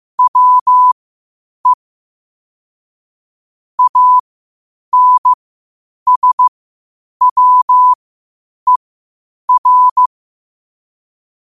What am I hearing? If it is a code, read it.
Morse code, heard as a keyed tone, about 15 wpm, spaced as for 8 wpm, 1000 Hz. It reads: WE ANSWER
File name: qotdfiles / 8wpm